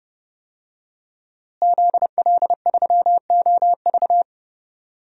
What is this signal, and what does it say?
Morse code, heard as a keyed tone, about 30 wpm, 700 Hz. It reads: ZL3OV
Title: DXCC-100 Call Signs - 5 Characters - Repeated 3x - 30wpm